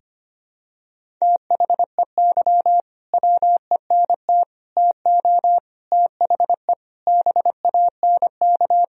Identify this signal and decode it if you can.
Morse code, heard as a keyed tone, about 25 wpm, 700 Hz. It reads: THEY WENT TO THE BANK
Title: Sentences from Top 1000 Words - Review Part 4 - 25wpm